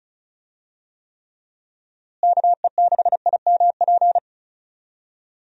Morse code, heard as a keyed tone, about 34 wpm, 700 Hz. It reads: KE6IMP
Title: USA Call Signs - 6 Characters - Repeated 3x - 35wpm